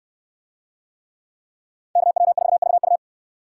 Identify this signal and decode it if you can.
Morse code, heard as a keyed tone, about 62 wpm, 700 Hz. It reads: KW4VU